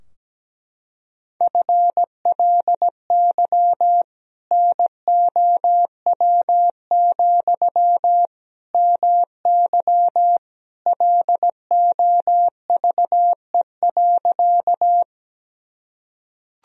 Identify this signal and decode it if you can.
Morse code, heard as a keyed tone, about 17 wpm, 700 Hz. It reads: FLY NOW, MY LOVE.
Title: Sentences from Top 1000 Words - Review Part 2 - 17wpm